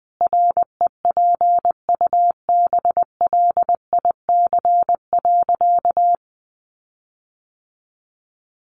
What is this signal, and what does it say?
Morse code, heard as a keyed tone, about 20 wpm, 700 Hz. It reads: REPUBLIC.
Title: Sentences from Top 1500 Words 20wpm